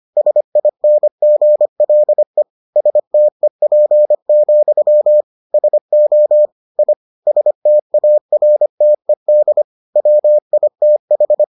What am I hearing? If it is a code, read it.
Morse code, heard as a keyed tone, about 25 wpm, 600 Hz. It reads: SINGLE STEP, SO I STARTED WITH